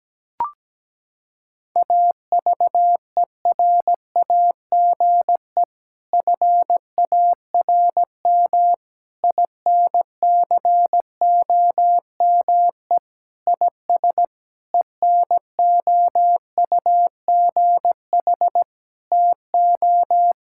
Morse code, heard as a keyed tone, about 17 wpm, 700 Hz. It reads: AVERAGE FARM INCOME IS ENOUGH TO
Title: Sentences from Top 1000 Words - Review Part 6 - 17wpm